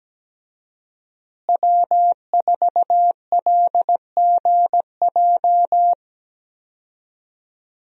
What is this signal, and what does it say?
Morse code, heard as a keyed tone, about 17 wpm, 700 Hz. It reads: W4LGJ
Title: USA Call Signs - 5 Characters - Repeated 2x - 17wpm